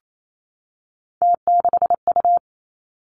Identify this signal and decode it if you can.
Morse code, heard as a keyed tone, about 28 wpm, 690 Hz. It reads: T6U
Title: Three Challenging Letter-Number Combinations 28wpm